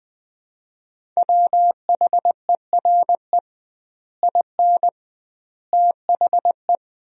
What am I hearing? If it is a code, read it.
Morse code, heard as a keyed tone, about 21 wpm, 700 Hz. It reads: WHERE IN THE
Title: 2x Word-Spacing - Sentences from Top 200 Words - 20wpm